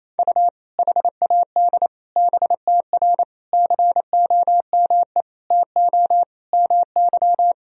Morse code, heard as a keyed tone, about 28 wpm, 700 Hz. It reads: U HAD BTR COME TO MY
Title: Sentences from Top 400 Words and Abbreviations 28wpm